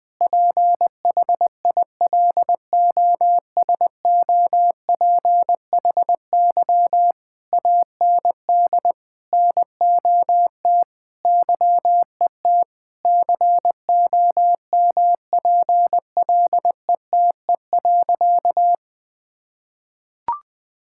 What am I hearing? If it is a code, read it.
Morse code, heard as a keyed tone, about 21 wpm, 700 Hz. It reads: PHILOSOPHY AND NOT YET COMPLETE.
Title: Sentences from Top 1000 Words - Review Part 6 - 20wpm